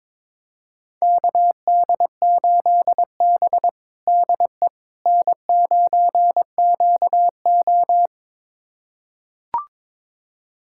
Morse code, heard as a keyed tone, about 22 wpm, 700 Hz. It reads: KD8B DE N9QO
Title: USA Callsign DE Callsign - 4 Characters - 22wpm